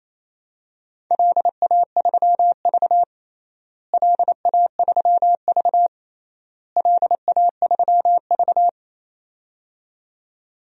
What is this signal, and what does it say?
Morse code, heard as a keyed tone, about 28 wpm, 700 Hz. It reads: LA3V LA3V LA3V
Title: DXCC-100 Call Signs - 4 Characters - Repeated 3x - 28wpm